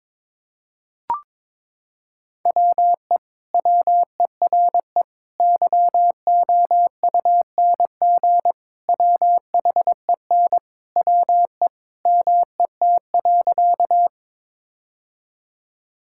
Morse code, heard as a keyed tone, about 22 wpm, 700 Hz. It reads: WE WERE YOUNG WHEN WE MET.
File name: practice / 22wpm / Sentences with 2-Word Phrases - 22wpm